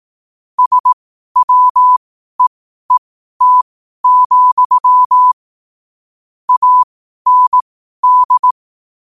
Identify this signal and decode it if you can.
Morse code, heard as a keyed tone, about 18 wpm, spaced as for 12 wpm, 1000 Hz. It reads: SWEET, AND
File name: qotdfiles / 12wpm